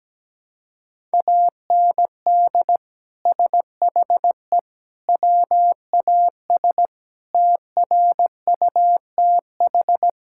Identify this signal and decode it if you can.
Morse code, heard as a keyed tone, about 17 wpm, 700 Hz. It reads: AND SHE WAS TRUTH
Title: Sentences from Top 1000 Words - Review Part 5 - 17wpm